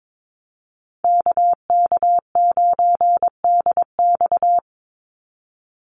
Morse code, heard as a keyed tone, about 22 wpm, 700 Hz. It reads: KK9DX
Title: Call Signs Repeated 2x - 22wpm